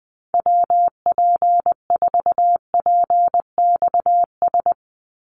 Morse code, heard as a keyed tone, about 20 wpm, 700 Hz. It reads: WP4PXS
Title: USA Call Signs 20wpm